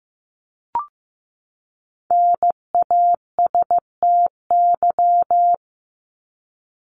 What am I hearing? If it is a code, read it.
Morse code, heard as a keyed tone, about 15 wpm, 700 Hz. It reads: NASTY
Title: Top 1751-2000 Words 15wpm